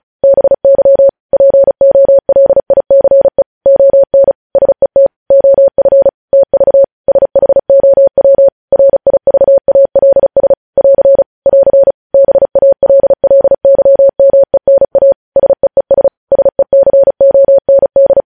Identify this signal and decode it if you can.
Morse code, heard as a keyed tone, about 36 wpm, 550 Hz. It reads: BY POLICE ON SET OF TV SHOW RIVALS <AR> <AR> BALLYMENA SEES SECOND
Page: news headlines in morse code at 35 wpm